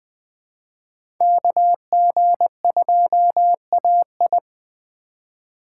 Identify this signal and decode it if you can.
Morse code, heard as a keyed tone, about 20 wpm, 700 Hz. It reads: KG2AI
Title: USA Call Signs - 5 Characters - 3x Character Spacing - 20wpm